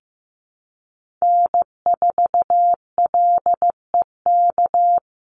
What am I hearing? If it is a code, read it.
Morse code, heard as a keyed tone, about 15 wpm, 700 Hz. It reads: N4LEK
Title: USA Call Signs 15wpm